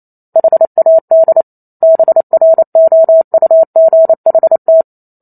Morse code, heard as a keyed tone, about 28 wpm, 650 Hz. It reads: HAD BROUGHT